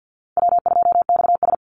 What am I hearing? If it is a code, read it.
Morse code, heard as a keyed tone, about 48 wpm, 700 Hz. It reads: W2XS
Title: Easy Call Signs Repeated 3x - 50wpm